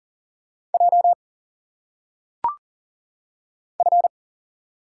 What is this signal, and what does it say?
Morse code, heard as a keyed tone, about 40 wpm, 700 Hz. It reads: J F
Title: Single Letters - Phonetic Pronunciation 40wpm